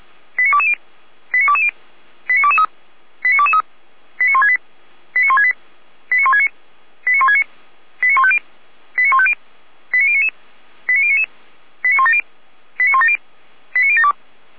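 Five-tone selective call (ZVEI), heard as quick runs of five tones, tones of about 70 ms. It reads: 89220 89220 89223 89223 89168 89168 89169 89169 89150 89150 89099 89099 89170 89170 89072
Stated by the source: ZVEI five-tone call received on an SDR, from the Feuerwehr Orlamuende (feuerwehr-orlamuende.org) audio archive.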